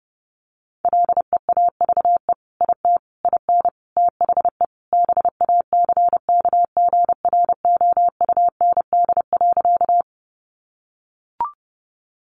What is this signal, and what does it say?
Morse code, heard as a keyed tone, about 30 wpm, 700 Hz. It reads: LEAVE IT IN THE BACKGROUND.
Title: Sentences from Top 1300 Words 30wpm